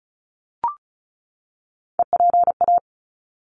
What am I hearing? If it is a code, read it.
Morse code, heard as a keyed tone, about 34 wpm, 700 Hz. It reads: EPA
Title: ARRL Sections 35wpm